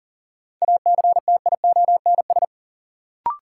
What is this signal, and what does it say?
Morse code, heard as a keyed tone, about 39 wpm, 700 Hz. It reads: ACTIONS
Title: Sentences from Top 1000 Words - Review Part 6 - New Word Drill - 40wpm